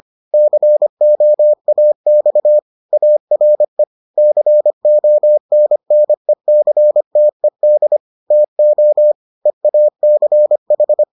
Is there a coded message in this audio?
Yes: Morse code from 0.3 to 11.0 s, heard as a keyed tone, about 25 wpm, 600 Hz. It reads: COAX ARE CONNECTED TO EACH